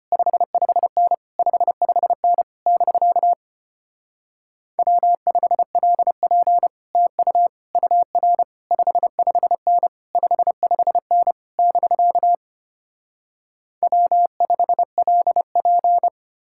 Morse code, heard as a keyed tone, about 33 wpm, 700 Hz. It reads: 55N 55N <BK> W5LP TU UR 55N 55N <BK> W5LP